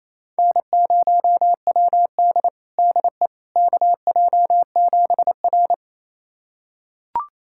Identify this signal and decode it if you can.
Morse code, heard as a keyed tone, about 28 wpm, 700 Hz. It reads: N0WD DE KJ7R
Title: USA Callsign DE Callsign - 4 Characters - 28wpm